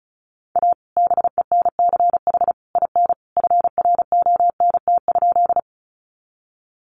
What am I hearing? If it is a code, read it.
Morse code, heard as a keyed tone, about 34 wpm, 700 Hz. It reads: A BENCH IN FRONT?
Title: Sentences from Top 1400 Words 35wpm